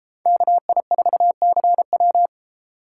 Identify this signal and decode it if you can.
Morse code, heard as a keyed tone, about 32 wpm, 700 Hz. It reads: KI4CW